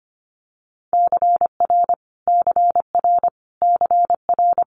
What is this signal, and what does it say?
Morse code, heard as a keyed tone, about 25 wpm, 700 Hz. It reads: CR CR CR
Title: Multi-letter Phonemes Repeated 3x 25wpm